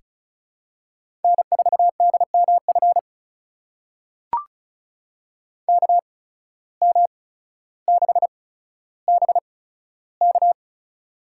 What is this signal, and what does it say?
Morse code, heard as a keyed tone, about 34 wpm, spaced as for 15 wpm, 700 Hz. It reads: N4DMF KM6BK
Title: USA Call Signs - 5 Characters - 8x Character Spacing - 35wpm